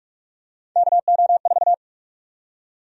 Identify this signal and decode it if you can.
Morse code, heard as a keyed tone, about 46 wpm, 700 Hz. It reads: KO4
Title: USA Call Sign Prefixes - 45wpm